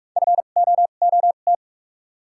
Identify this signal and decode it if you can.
Morse code, heard as a keyed tone, about 45 wpm, spaced as for 32 wpm, 700 Hz. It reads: FOOT
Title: 2x Character Spacing - Top 500 Words - 45wpm